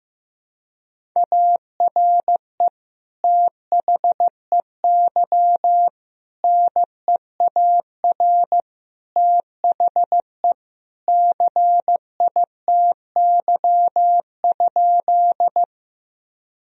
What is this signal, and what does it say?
Morse code, heard as a keyed tone, about 16 wpm, 700 Hz. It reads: ARE THEY NEAR THE CITY?
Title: Sentences from Top 1000 Words - Review Part 1 - 15wpm